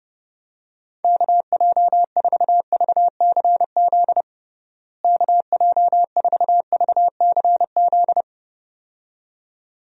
Morse code, heard as a keyed tone, about 30 wpm, 700 Hz. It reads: KJ4VCZ KJ4VCZ
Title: USA Call Signs - 6 Characters - Repeated 2x - 30wpm